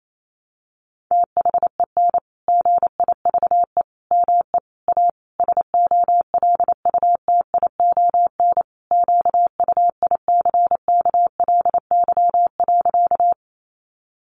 Morse code, heard as a keyed tone, about 28 wpm, 700 Hz. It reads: THEN GIVE ME A SOLUTION QUICKLY.
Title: Sentences from Top 1300 Words 28wpm